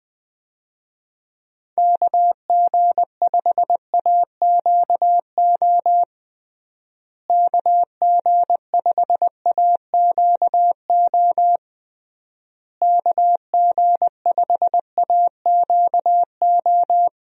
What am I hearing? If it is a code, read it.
Morse code, heard as a keyed tone, about 20 wpm, 700 Hz. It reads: KG5AQO KG5AQO KG5AQO
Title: USA Call Signs - 6 Characters - Repeated 3x - 20wpm